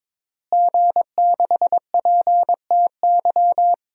Morse code, heard as a keyed tone, about 22 wpm, 700 Hz. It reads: G6PTY
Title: DXCC-100 Call Signs - 5 Characters - Repeated 2x - 22wpm